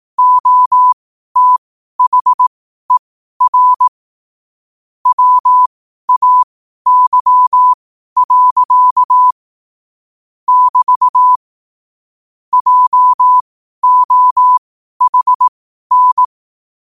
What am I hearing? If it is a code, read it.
Morse code, heard as a keyed tone, about 18 wpm, spaced as for 12 wpm, 1000 Hz. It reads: OTHER WAY. <BT> JOHN
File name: qotdfiles / 12wpm